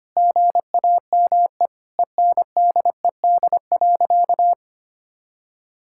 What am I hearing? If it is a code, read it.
Morse code, heard as a keyed tone, about 25 wpm, 700 Hz. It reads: GAME ENDED.
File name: practice / 25wpm / Sentences with 2-Word Phrases - 25wpm